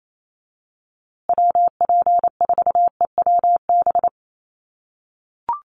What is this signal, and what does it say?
Morse code, heard as a keyed tone, about 28 wpm, 700 Hz. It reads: WP4EWB
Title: USA Call Signs Repeated 2x 28wpm